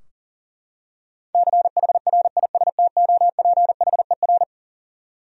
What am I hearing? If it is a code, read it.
Morse code, heard as a keyed tone, about 38 wpm, 700 Hz. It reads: CHRISTOPHER
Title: USA Common Names - 40wpm